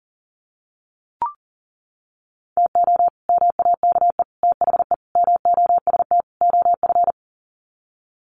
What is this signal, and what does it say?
Morse code, heard as a keyed tone, about 40 wpm, 700 Hz. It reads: TO MAKE THE MOST OF
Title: Sets of 5 Words Encore2 40wpm